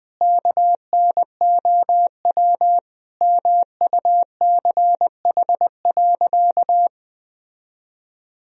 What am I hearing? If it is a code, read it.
Morse code, heard as a keyed tone, about 20 wpm, 700 Hz. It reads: KNOW MUCH.
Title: Sentences from Top 1000 Words - Review Part 3 - 20wpm